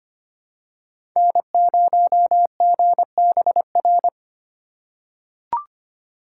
Morse code, heard as a keyed tone, about 25 wpm, 700 Hz. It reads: N0GBR
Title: USA Call Signs - 5 Characters - 4x Character Spacing - 25wpm